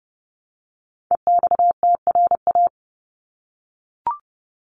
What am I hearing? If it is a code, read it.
Morse code, heard as a keyed tone, about 30 wpm, 700 Hz. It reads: EXTRA
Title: Top 751-1000 Words 30wpm